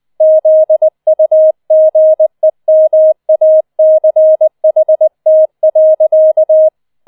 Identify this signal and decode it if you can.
Morse code, heard as a keyed tone, about 20 wpm, 630 Hz. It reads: ZUGEMACHT.